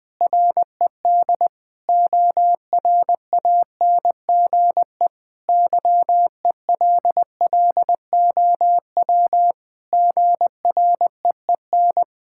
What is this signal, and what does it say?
Morse code, heard as a keyed tone, about 21 wpm, 700 Hz. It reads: RED ORANGE YELLOW GREEN
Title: Pangram Challenge v2 - 20wpm